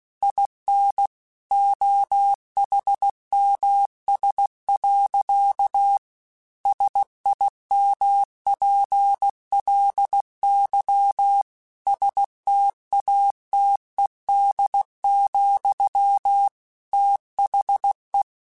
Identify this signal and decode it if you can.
Morse code, heard as a keyed tone, about 16 wpm, 800 Hz. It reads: IN OHMS. SIMPLY STATED, THE
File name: Technical Subjects / Technical Subjects-mp3s / XL @15